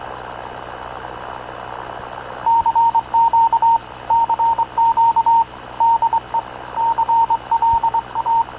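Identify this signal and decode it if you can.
Morse code, heard as a keyed tone, about 22 wpm, 920 Hz, with radio noise. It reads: CQ CQ DE CLA
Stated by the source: Havana Radio / CLA sending Spanish press (PX) in Morse, recorded by Radio Officer Paul Dumesnil onboard CNAV QUEST/CZDO.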